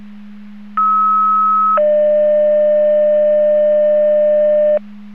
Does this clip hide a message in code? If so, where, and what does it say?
Yes: two-tone page from 0.8 to 4.8 s, heard as a one-second tone, then a three-second tone, about 1290 then 600 Hz.